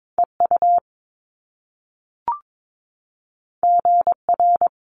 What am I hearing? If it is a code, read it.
Morse code, heard as a keyed tone, about 22 wpm, 700 Hz. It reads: EU GR
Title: Top Two Letter Combinations in Words 22wpm